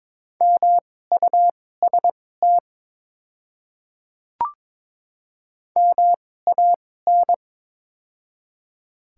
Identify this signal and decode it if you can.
Morse code, heard as a keyed tone, about 22 wpm, spaced as for 16 wpm, 690 Hz. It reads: MUST MAN
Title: 2x Character Spacing - Top 200 Words - 22wpm